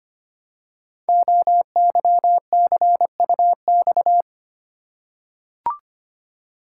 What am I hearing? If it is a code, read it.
Morse code, heard as a keyed tone, about 25 wpm, 700 Hz. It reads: OYCUX
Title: Code Groups - 5 Letter:Numbers - 25wpm